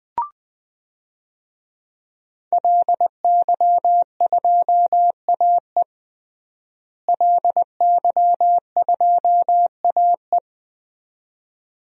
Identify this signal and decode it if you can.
Morse code, heard as a keyed tone, about 20 wpm, 700 Hz. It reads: LY2AE LY2AE
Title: DXCC-100 Call Signs - 5 Characters - Repeated 2x - 20wpm